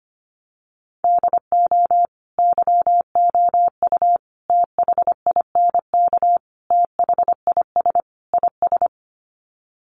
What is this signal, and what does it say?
Morse code, heard as a keyed tone, about 25 wpm, 700 Hz. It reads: DO YOU THINK THIS IS
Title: Sets of 5 Words Encore2 25wpm